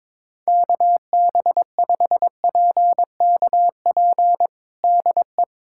Morse code, heard as a keyed tone, about 22 wpm, 700 Hz. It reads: KB5PKP DE
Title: USA Callsign DE Callsign - 6 Characters - 22wpm